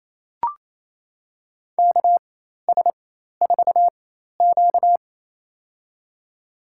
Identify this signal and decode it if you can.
Morse code, heard as a keyed tone, about 28 wpm, spaced as for 13 wpm, 700 Hz. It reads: KS4Q
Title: USA Call Signs - 4x Character Spacing - 28wpm